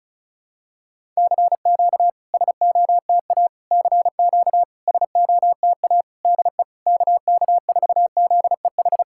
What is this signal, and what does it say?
Morse code, heard as a keyed tone, about 34 wpm, 690 Hz. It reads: CQ SOTA CQ SOTA DE KK4ZEH
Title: SOTA QSO Practice 35wpm